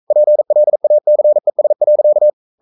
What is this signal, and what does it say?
Morse code, heard as a keyed tone, about 42 wpm, 600 Hz. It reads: PLACES.
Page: Romanen Never af Ken Follet på engelsk.